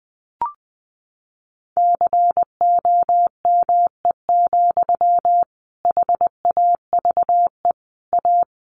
Morse code, heard as a keyed tone, about 20 wpm, 700 Hz. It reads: COME, HAVE A
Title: Sentences from Top 1500 Words 20wpm